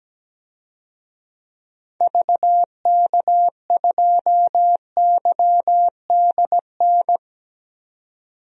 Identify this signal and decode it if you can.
Morse code, heard as a keyed tone, about 17 wpm, 700 Hz. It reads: VK2YDN
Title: DXCC-100 Call Signs - 6 Characters - 17wpm